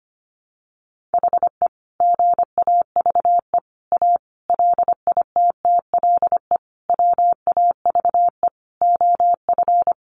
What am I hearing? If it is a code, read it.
Morse code, heard as a keyed tone, about 25 wpm, 700 Hz. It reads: HE GAVE A LITTLE WAVE OF
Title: Sentences from Top 1400 Words 25wpm